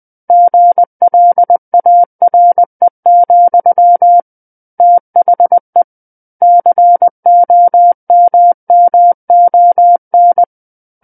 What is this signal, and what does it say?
Morse code, heard as a keyed tone, about 20 wpm, 700 Hz. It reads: GLARE, THE COMMON
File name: 20 WPM